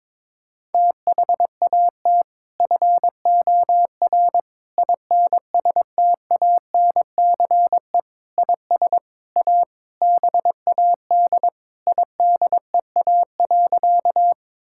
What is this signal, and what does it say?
Morse code, heard as a keyed tone, about 22 wpm, 700 Hz. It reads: THAT FOR INSTANCE IS A BAD IDEA.
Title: Sentences from Top 1000 Words - Review Part 5 - 22wpm